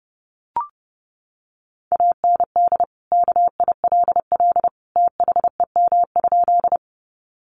Morse code, heard as a keyed tone, about 30 wpm, 700 Hz. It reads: AND KILL THEM?
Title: Sentences from Top 1400 Words 30wpm